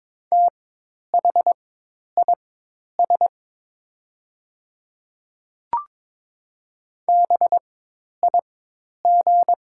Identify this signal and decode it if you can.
Morse code, heard as a keyed tone, about 23 wpm, spaced as for 10 wpm, 690 Hz. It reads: THIS BIG
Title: 4x Character Spacing - Top 200 Words - 22wpm